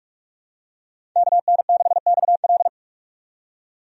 Morse code, heard as a keyed tone, about 46 wpm, 700 Hz. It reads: KN6XL
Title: USA Call Signs - 5 Characters - 45wpm